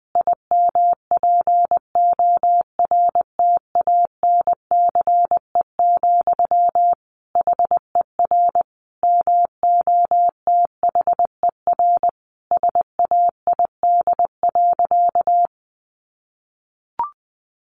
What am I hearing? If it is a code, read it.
Morse code, heard as a keyed tone, about 20 wpm, 700 Hz. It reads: IMPORTANCE, HER MOTHER SAID.
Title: Sentences from Top 1500 Words 20wpm